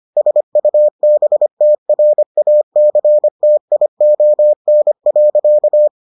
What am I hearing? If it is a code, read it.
Morse code, heard as a keyed tone, about 25 wpm, 600 Hz. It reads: SUBTRACTION.